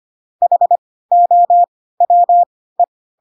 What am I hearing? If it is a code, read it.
Morse code, heard as a keyed tone, about 25 wpm, spaced as for 16 wpm, 700 Hz. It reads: HOWE
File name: Pinocchio / Pinocchio ch4 10 25 ik1hge